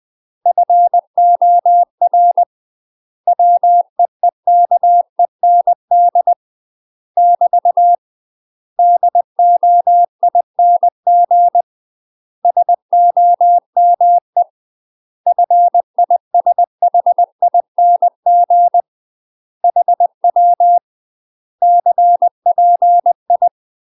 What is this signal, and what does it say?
Morse code, heard as a keyed tone, about 20 wpm, 700 Hz. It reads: FOR WEEKEND <BT> DOING SOME FISHING HW CPI
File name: UK MASTER QSOs TRAINING / 20WPM / UK MASTER QSOs 0073